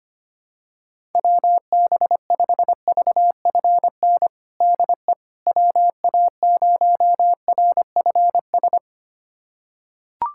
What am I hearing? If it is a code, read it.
Morse code, heard as a keyed tone, about 25 wpm, 700 Hz. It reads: WB5VFN DE WA0RFS
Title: USA Callsign DE Callsign - 6 Characters - 25wpm